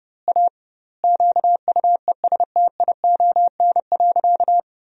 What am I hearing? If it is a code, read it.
Morse code, heard as a keyed tone, about 30 wpm, 700 Hz. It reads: A QUESTION.
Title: 2x Word-Spacing - Sentences from Top 400 Words - 30wpm